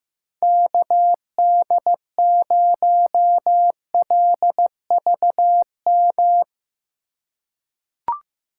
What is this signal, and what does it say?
Morse code, heard as a keyed tone, about 16 wpm, 700 Hz. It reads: KD0LVM
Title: USA Call Signs - 6 Characters - 4x Character Spacing - 15wpm